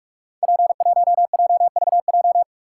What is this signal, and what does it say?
Morse code, heard as a keyed tone, about 44 wpm, 700 Hz. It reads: P1JVJ
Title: Code Groups - 5 Letter:Numbers - 45wpm